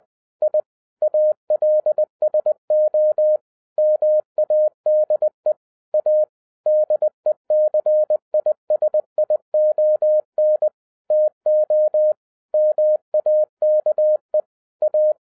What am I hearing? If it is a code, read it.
Morse code, heard as a keyed tone, about 20 wpm, 610 Hz. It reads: I ALSO MADE A DECISION TO MAKE A
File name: 2012-01-mp3 / n05-CAME BACK TO HAM RADIO AFTER 30 YEARS OF INTERVAL-20wpm